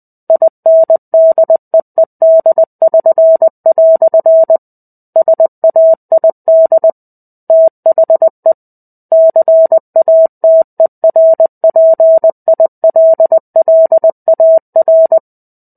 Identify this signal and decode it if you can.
Morse code, heard as a keyed tone, about 20 wpm, 650 Hz. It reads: INDEED<SN>" SAID THE CATERPILLAR
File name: 05-Chapter-05 20 WPM